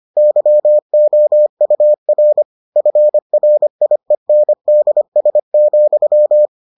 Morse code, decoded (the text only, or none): YOUR FRIENDS,